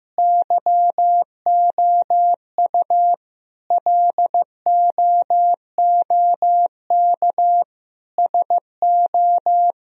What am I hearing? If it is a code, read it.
Morse code, heard as a keyed tone, about 16 wpm, 700 Hz. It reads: YOU LOOK SO
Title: Sentences from Top 1000 Words - Review Part 1 - 15wpm